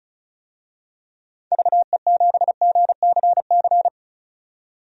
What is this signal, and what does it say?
Morse code, heard as a keyed tone, about 36 wpm, 700 Hz. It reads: VE7GCC
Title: DXCC-100 Call Signs - 6 Characters - 35wpm